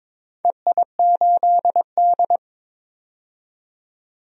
Morse code, heard as a keyed tone, about 22 wpm, 700 Hz. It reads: EI8D
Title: DXCC-100 Call Signs - 4 Characters - 22wpm